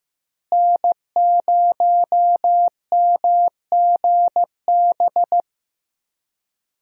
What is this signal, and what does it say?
Morse code, heard as a keyed tone, about 15 wpm, 700 Hz. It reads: N0MGB
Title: USA Call Signs - 5 Characters - Repeated 3x - 15wpm